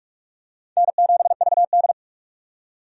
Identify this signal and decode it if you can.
Morse code, heard as a keyed tone, about 46 wpm, 700 Hz. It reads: N7VD
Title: USA Call Signs - 4 Characters - 45wpm